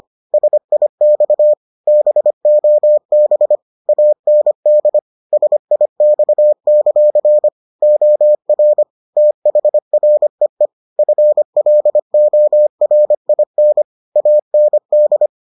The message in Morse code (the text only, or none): SIX BOB AND SIX; OR THREE FLORIN AND